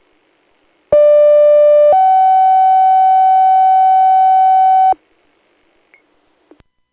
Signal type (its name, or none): two-tone page